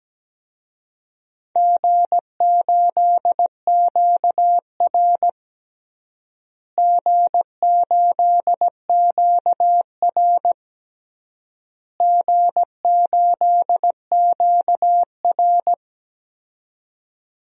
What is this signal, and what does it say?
Morse code, heard as a keyed tone, about 17 wpm, 700 Hz. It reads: G8QR G8QR G8QR
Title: DXCC-100 Call Signs - 4 Characters - Repeated 3x - 17wpm